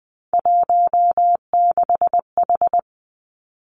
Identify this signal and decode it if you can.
Morse code, heard as a keyed tone, about 20 wpm, 700 Hz. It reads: 16H
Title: Three Challenging Letter-Number Combinations 20wpm